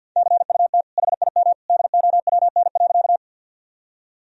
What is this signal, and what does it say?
Morse code, heard as a keyed tone, about 50 wpm, 700 Hz. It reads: CUT HIM DOWN.
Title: Sentences from Top 1000 Words - Review Part 1 - 50wpm